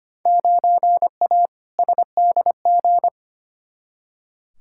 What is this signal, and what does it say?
Morse code, heard as a keyed tone, about 25 wpm, 700 Hz. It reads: 9A SDG
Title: Field Day Exchanges Encore - 25wpm